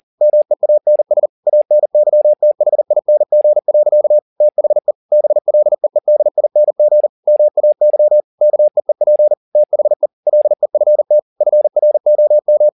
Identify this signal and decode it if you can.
Morse code, heard as a keyed tone, about 40 wpm, 600 Hz. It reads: MEANS ANYTHING. THE BLEEDING MAY KEEP THE LEFT FROM
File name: the old man and the sea 2 40wpm